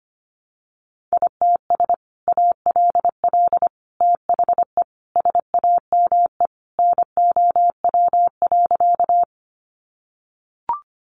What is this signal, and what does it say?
Morse code, heard as a keyed tone, about 25 wpm, 700 Hz. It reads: ITS ALL THE SAME NOW.
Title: Sentences from Top 600 Words 25wpm